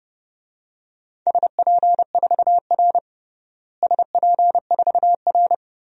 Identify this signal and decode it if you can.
Morse code, heard as a keyed tone, about 30 wpm, 700 Hz. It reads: SP4R SP4R
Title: DXCC-100 Call Signs - 4 Characters - Repeated 2x - 30wpm